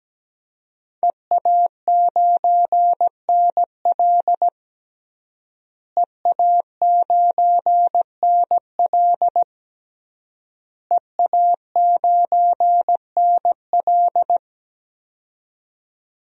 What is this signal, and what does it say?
Morse code, heard as a keyed tone, about 17 wpm, 700 Hz. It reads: EA9NL EA9NL EA9NL
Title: DXCC-100 Call Signs - 5 Characters - Repeated 3x - 17wpm